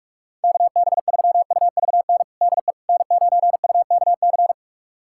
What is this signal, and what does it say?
Morse code, heard as a keyed tone, about 41 wpm, 700 Hz. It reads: KB3UVN DE N9UKC
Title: USA Callsign DE Callsign 45wpm